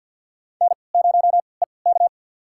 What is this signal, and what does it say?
Morse code, heard as a keyed tone, about 47 wpm, spaced as for 28 wpm, 700 Hz. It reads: N0EK
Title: USA Call Signs - 4 Characters - 3x Character Spacing - 50wpm